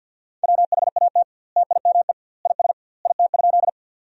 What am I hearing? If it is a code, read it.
Morse code, heard as a keyed tone, about 50 wpm, 700 Hz. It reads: WHAT TIME IS IT?
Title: 2x Word-Spacing - Sentences from Top 100 Words - 50wpm